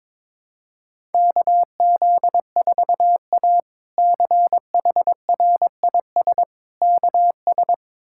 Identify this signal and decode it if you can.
Morse code, heard as a keyed tone, about 21 wpm, 700 Hz. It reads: KZ4A CHRIS KS
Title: K1USN SST QSO Practice - 22wpm